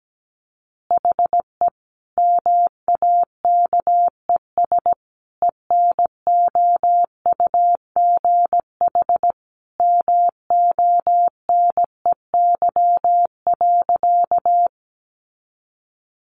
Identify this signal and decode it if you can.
Morse code, heard as a keyed tone, about 17 wpm, 700 Hz. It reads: HE MAKES ENOUGH MONEY.
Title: Additional Easy Phrases 17wpm